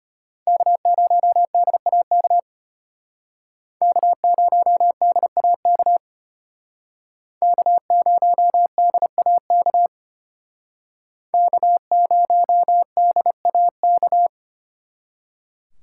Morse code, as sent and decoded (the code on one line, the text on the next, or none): -.- ----- -.. .- -.- / -.- ----- -.. .- -.- / -.- ----- -.. .- -.- / -.- ----- -.. .- -.-
K0DAK K0DAK K0DAK K0DAK